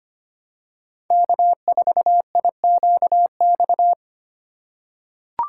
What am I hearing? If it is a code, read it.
Morse code, heard as a keyed tone, about 25 wpm, 700 Hz. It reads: K4IQX
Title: USA Call Signs - 5 Characters - 3x Character Spacing - 25wpm